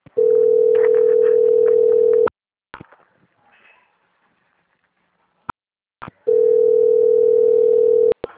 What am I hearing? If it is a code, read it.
Ringback tone (North America), 440 + 480 Hz.